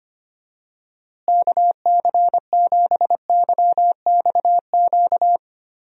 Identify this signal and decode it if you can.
Morse code, heard as a keyed tone, about 25 wpm, 700 Hz. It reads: KC7YXQ
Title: USA Call Signs - 6 Characters - Repeated 3x - 25wpm